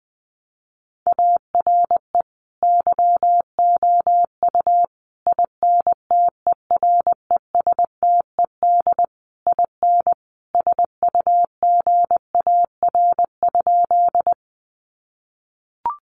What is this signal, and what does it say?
Morse code, heard as a keyed tone, about 20 wpm, 700 Hz. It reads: ARE YOU INTERESTED IN SUGAR?
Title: Sentences from Top 1200 Words 20wpm